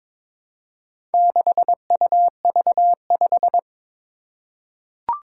Morse code, heard as a keyed tone, about 23 wpm, 700 Hz. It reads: 6UV5
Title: Four Challenging Letter-Number Combinations 22wpm